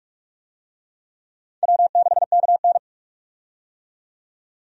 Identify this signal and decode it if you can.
Morse code, heard as a keyed tone, about 47 wpm, 700 Hz. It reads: W6KN
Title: USA Call Signs - 4 Characters - 45wpm